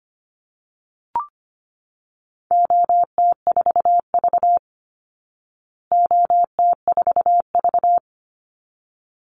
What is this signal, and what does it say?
Morse code, heard as a keyed tone, about 25 wpm, 700 Hz. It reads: OT4V OT4V
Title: Call Signs Repeated 3x - 25wpm